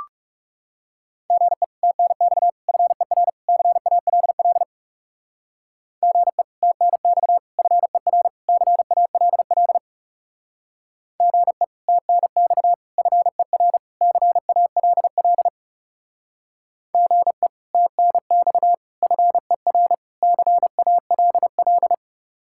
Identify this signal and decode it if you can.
Morse code, heard as a keyed tone, about 36 wpm, 700 Hz. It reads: GE TNX FER CALL GE TNX FER CALL GE TNX FER CALL GE TNX FER CALL
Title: Speed-Racing Common QSO Phrases Encore - 30wpm